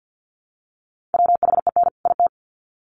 Morse code, heard as a keyed tone, about 50 wpm, 700 Hz. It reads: WHEN IT
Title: Sets of 2 Words 50wpm